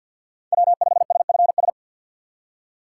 Morse code, heard as a keyed tone, about 53 wpm, 700 Hz. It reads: W5SFS
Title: USA Call Signs - 6x Character Spacing - 50wpm